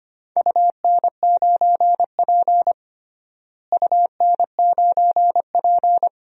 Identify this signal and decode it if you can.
Morse code, heard as a keyed tone, about 25 wpm, 700 Hz. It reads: UN9P UN9P
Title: DXCC-100 Call Signs - 4 Characters - Repeated 2x - 25wpm